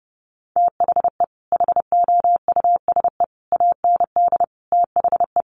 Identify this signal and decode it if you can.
Morse code, heard as a keyed tone, about 30 wpm, 700 Hz. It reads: THE HOUSE AND THE
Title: Sets of 5 Words Encore2 30wpm